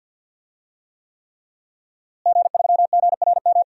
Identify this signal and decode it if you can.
Morse code, heard as a keyed tone, about 52 wpm, 700 Hz. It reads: G3GRM
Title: DXCC-100 Call Signs - 5 Characters - Repeated 3x - 50wpm